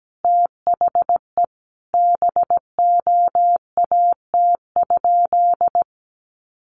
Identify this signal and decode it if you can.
Morse code, heard as a keyed tone, about 17 wpm, 700 Hz. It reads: THE BOAT?
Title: Sentences from Top 1300 Words 17wpm